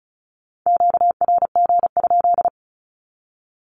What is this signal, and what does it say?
Morse code, heard as a keyed tone, about 34 wpm, 700 Hz. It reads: QRG?
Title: Q-Codes 35wpm